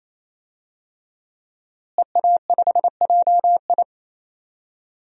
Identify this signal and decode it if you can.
Morse code, heard as a keyed tone, about 28 wpm, 700 Hz. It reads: EA5JI
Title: DXCC-100 Call Signs - 5 Characters - Repeated 3x - 28wpm